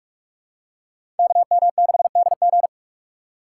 Morse code, heard as a keyed tone, about 45 wpm, 700 Hz. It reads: KM6DG
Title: USA Call Signs - 5 Characters - 6x Character Spacing - 45wpm